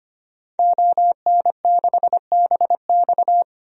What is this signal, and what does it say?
Morse code, heard as a keyed tone, about 25 wpm, 700 Hz. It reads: ON6BX
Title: DXCC-100 Call Signs - 5 Characters - 25wpm